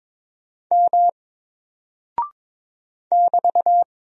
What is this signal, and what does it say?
Morse code, heard as a keyed tone, about 22 wpm, 700 Hz. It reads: M <BT>
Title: Single Character (Letter:Number:Punctuation:Prosign) Alt 22wpm